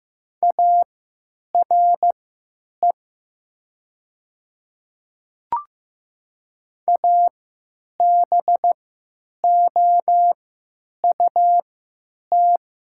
Morse code, heard as a keyed tone, about 15 wpm, spaced as for 9 wpm, 700 Hz. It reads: ARE ABOUT
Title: 3x Character Spacing - Top 200 Words - 15wpm